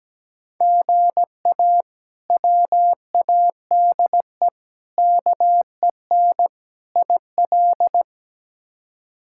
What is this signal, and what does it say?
Morse code, heard as a keyed tone, about 17 wpm, 700 Hz. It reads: GA WADE KEN IL
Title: K1USN SST QSO Practice - 17wpm